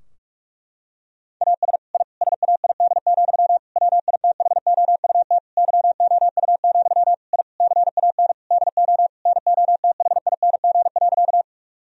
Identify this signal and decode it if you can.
Morse code, heard as a keyed tone, about 46 wpm, 700 Hz. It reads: AS I SAID, WITHOUT YOU, I CAN DO NOTHING.
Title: Sentences from Top 1000 Words - Review Part 3 - 45wpm